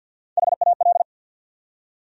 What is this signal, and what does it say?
Morse code, heard as a keyed tone, about 53 wpm, 700 Hz. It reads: HAL
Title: USA Common Names - Max 3 Letters - 50wpm